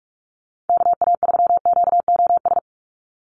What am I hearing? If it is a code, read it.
Morse code, heard as a keyed tone, about 46 wpm, 690 Hz. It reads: KA3QOS
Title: USA Call Signs 45wpm